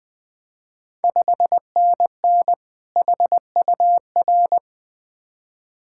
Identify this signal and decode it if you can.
Morse code, heard as a keyed tone, about 20 wpm, 700 Hz. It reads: 5NN HUR
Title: Ontario QSO Party Exchange 20wpm